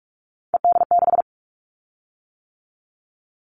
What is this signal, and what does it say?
Morse code, heard as a keyed tone, about 46 wpm, 700 Hz. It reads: ED6
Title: Callsign Prefixes for Top 100 DXCC - 45wpm